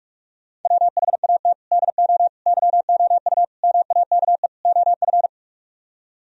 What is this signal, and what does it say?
Morse code, heard as a keyed tone, about 49 wpm, 700 Hz. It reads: WHAT DO YOU MAKE OF